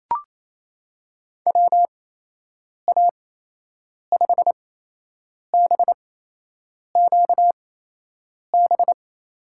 Morse code, heard as a keyed tone, about 28 wpm, spaced as for 8 wpm, 700 Hz. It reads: WA5BQB
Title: USA Call Signs - 6 Characters - 8x Character Spacing - 28wpm